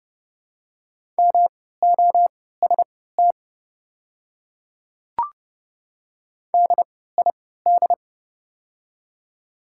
Morse code, heard as a keyed tone, about 32 wpm, spaced as for 17 wpm, 700 Hz. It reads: MOST DID
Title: 3x Character Spacing - Top 200 Words - 30wpm